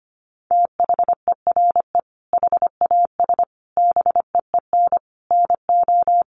Morse code, heard as a keyed tone, about 25 wpm, 700 Hz. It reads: THERE HAS BEEN NO
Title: Sentences from Top 700 Words 25wpm